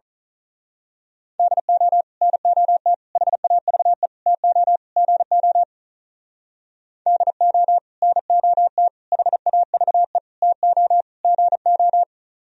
Morse code, heard as a keyed tone, about 38 wpm, 700 Hz. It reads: DO NOT HAVE TO GO DO NOT HAVE TO GO
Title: Speed-Racing - Sets of 5 Words 35wpm